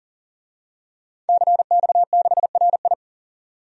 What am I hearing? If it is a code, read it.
Morse code, heard as a keyed tone, about 40 wpm, 700 Hz. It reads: CX6RI
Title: DXCC-100 Call Signs - 5 Characters - 40wpm